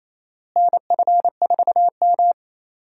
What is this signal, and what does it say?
Morse code, heard as a keyed tone, about 28 wpm, 700 Hz. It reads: NF4M
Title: USA Call Signs - 4 Characters - Repeated 2x - 28wpm